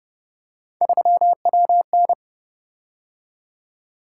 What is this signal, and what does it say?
Morse code, heard as a keyed tone, about 29 wpm, 700 Hz. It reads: 3WN